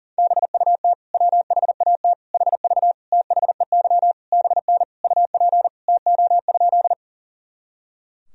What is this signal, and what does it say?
Morse code, heard as a keyed tone, about 40 wpm, 700 Hz. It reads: BUT WHAT HV THEY BN UP TO?
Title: Sentences from Top 100 Words and Abbreviations 40wpm